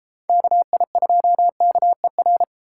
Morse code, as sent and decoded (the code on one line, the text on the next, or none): -.- .. ..--- -.- . .-.
KI2KER